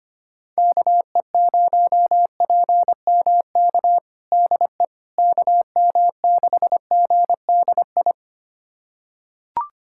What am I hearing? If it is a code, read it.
Morse code, heard as a keyed tone, about 25 wpm, 700 Hz. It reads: KE0PMK DE KM6GDI
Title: USA Callsign DE Callsign - 6 Characters - 25wpm